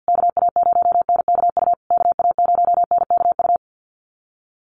Morse code, heard as a keyed tone, about 50 wpm, 700 Hz. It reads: KA0NKU KA0NKU
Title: USA Call Signs Repeated 3x 50wpm